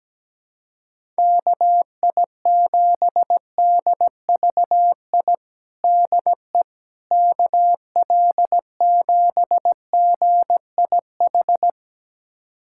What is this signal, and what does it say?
Morse code, heard as a keyed tone, about 17 wpm, 700 Hz. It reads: KI7DVI DE KL7GIH
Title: USA Callsign DE Callsign - 6 Characters - 17wpm